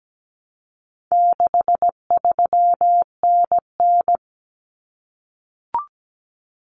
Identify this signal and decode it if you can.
Morse code, heard as a keyed tone, about 17 wpm, 700 Hz. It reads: 63NN
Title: Common Cut Numbers - 4-Digit 17wpm